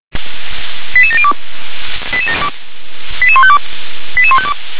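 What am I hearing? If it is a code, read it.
Five-tone selective call (ZVEI), heard as quick runs of five tones, tones of about 70 ms. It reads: 88702 88152 88152
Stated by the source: ZVEI five-tone call 88152, received on SDR.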